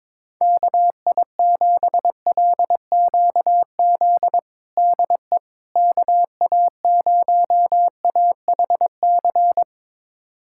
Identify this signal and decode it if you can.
Morse code, heard as a keyed tone, about 22 wpm, 700 Hz. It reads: KI7LQZ DE KA0AHC
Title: USA Callsign DE Callsign 22wpm